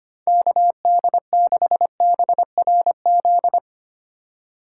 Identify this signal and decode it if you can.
Morse code, heard as a keyed tone, about 32 wpm, 700 Hz. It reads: KD6BRZ